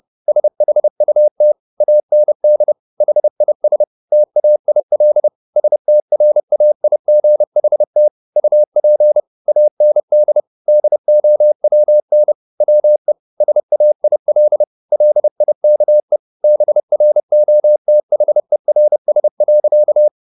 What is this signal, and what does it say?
Morse code, heard as a keyed tone, about 30 wpm, 600 Hz. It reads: SHUT AND HIS TAIL STRAIGHT UP AND DOWN WE SAIL LIKE BROTHERS.